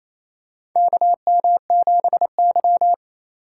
Morse code, heard as a keyed tone, about 28 wpm, 700 Hz. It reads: KM7Y
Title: USA Call Signs - 4 Characters - 4x Character Spacing - 28wpm